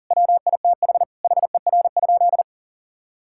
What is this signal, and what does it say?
Morse code, heard as a keyed tone, about 40 wpm, 690 Hz. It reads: WITH HER?
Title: Sentences from Top 1000 Words - Review Part 5 - 40wpm